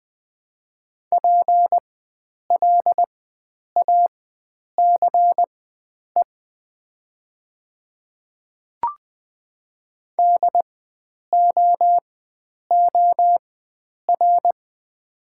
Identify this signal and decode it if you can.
Morse code, heard as a keyed tone, about 20 wpm, spaced as for 9 wpm, 700 Hz. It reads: PLACE DOOR
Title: 4x Character Spacing - Top 300 Words - 20wpm